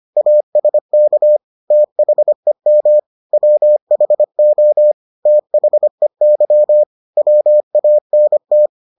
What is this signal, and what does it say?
Morse code, heard as a keyed tone, about 25 wpm, 600 Hz. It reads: ASK THEM WHO THEY WANT